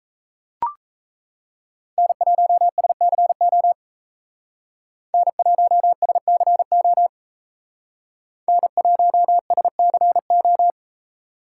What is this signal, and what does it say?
Morse code, heard as a keyed tone, about 36 wpm, 700 Hz. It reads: N1SCO N1SCO N1SCO